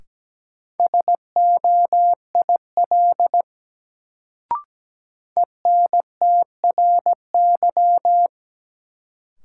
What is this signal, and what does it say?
Morse code, heard as a keyed tone, about 17 wpm, 700 Hz. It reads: SOIL ENTRY
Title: Sentences from Top 1500 Words - New Word Drill Alt - 17wpm